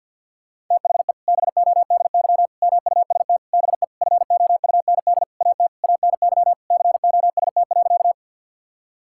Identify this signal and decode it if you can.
Morse code, heard as a keyed tone, about 49 wpm, 700 Hz. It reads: THE BODY MUST BE FOUND AT ANY COST.
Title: Sentences from Top 1000 Words - Review Part 5 - 50wpm